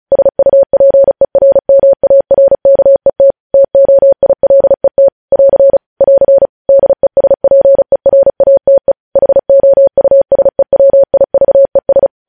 Morse code, heard as a keyed tone, about 36 wpm, 550 Hz. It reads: SUPERMARKET TOILET <AR> <AR> DESPERATE HOUSEWIVES
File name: podcast / News headlines-1x-35x35